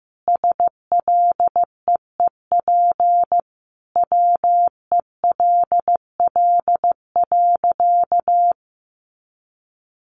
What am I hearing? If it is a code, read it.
Morse code, heard as a keyed tone, about 15 wpm, 700 Hz. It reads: SLEEP WELL.
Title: Sentences from Top 500 Words 15wpm